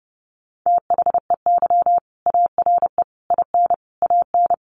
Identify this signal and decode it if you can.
Morse code, heard as a keyed tone, about 30 wpm, 700 Hz. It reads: THEY ARE IN AN
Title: Sentences from Top 1000 Words 30wpm